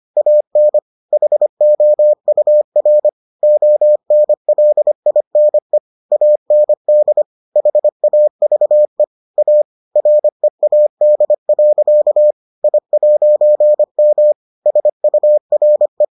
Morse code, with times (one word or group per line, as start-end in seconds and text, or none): 0.2-0.8 AN
1.1-3.1 HOUR
3.4-5.8 ONLINE
6.1-7.2 AND
7.6-9.0 HAVE
9.4-9.6 A
10.0-12.3 READ.
12.6-14.3 I'M
14.7-16.0 SURE